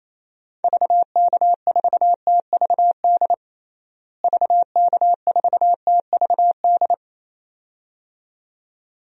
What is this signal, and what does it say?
Morse code, heard as a keyed tone, about 28 wpm, 700 Hz. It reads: VK4TVD VK4TVD
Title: DXCC-100 Call Signs - 6 Characters - Repeated 2x - 28wpm